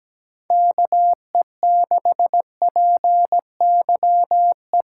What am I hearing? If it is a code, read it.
Morse code, heard as a keyed tone, about 17 wpm, 700 Hz. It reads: KE6PYE
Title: USA Call Signs - 6 Characters - 4x Character Spacing - 17wpm